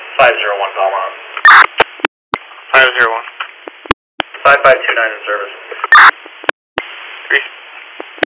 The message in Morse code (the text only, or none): none